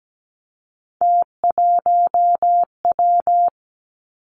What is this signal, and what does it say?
Morse code, heard as a keyed tone, about 17 wpm, 700 Hz. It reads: T1W
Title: Three Challenging Letter-Number Combinations 17wpm